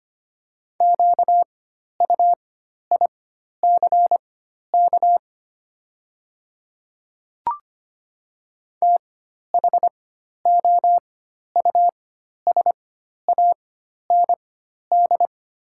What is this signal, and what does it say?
Morse code, heard as a keyed tone, about 25 wpm, spaced as for 12 wpm, 690 Hz. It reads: QUICK THOUSAND
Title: 4x Character Spacing - Top 500 Words - 25wpm